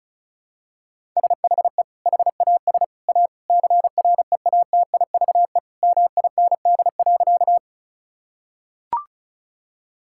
Morse code, heard as a keyed tone, about 34 wpm, 700 Hz. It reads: SHE HAS A CREATIVE MIND.
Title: Sentences from Top 1000 Words - Review Part 5 - 35wpm